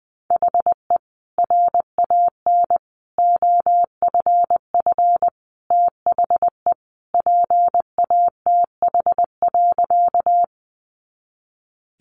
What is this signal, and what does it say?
Morse code, heard as a keyed tone, about 20 wpm, 700 Hz. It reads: HE RAN OFF THE PATH.
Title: A Sound Of Thunder 20wpm